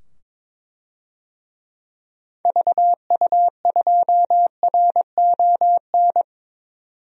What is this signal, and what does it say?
Morse code, heard as a keyed tone, about 22 wpm, 700 Hz. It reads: VU2RON
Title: DXCC-100 Call Signs - 6 Characters - 22wpm